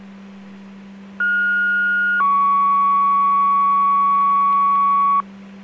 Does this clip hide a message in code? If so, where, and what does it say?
Yes: two-tone page from 1.2 to 5.2 s, heard as a one-second tone, then a three-second tone, about 1430 then 1120 Hz.